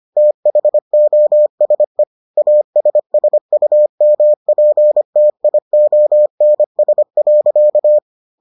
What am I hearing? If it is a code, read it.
Morse code, heard as a keyed tone, about 25 wpm, 600 Hz. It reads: THOSE ASSUMPTIONS.